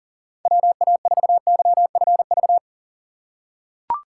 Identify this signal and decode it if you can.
Morse code, heard as a keyed tone, about 43 wpm, 690 Hz. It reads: WA4YFV
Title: USA Call Signs - 2x Character Spacing - 40wpm